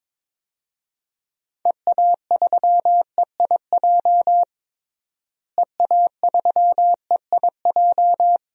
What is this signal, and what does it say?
Morse code, heard as a keyed tone, about 22 wpm, 700 Hz. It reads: EA3EIJ EA3EIJ
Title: DXCC-100 Call Signs - 6 Characters - Repeated 3x - 22wpm